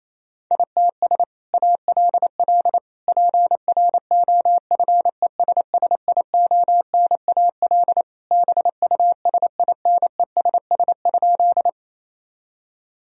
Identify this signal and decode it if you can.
Morse code, heard as a keyed tone, about 27 wpm, 700 Hz. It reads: ITS ALL PROFESSIONAL BUSINESS?
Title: Sentences from Top 1000 Words - Review Part 4 - 28wpm